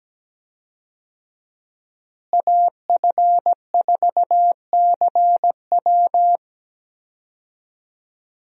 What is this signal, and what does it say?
Morse code, heard as a keyed tone, about 17 wpm, 700 Hz. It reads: AF4CW
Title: USA Call Signs - 5 Characters - Repeated 2x - 17wpm